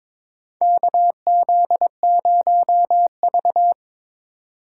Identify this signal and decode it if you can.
Morse code, heard as a keyed tone, about 22 wpm, 700 Hz. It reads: KZ0V
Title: USA Call Signs - 4 Characters - 2x Character Spacing - 22wpm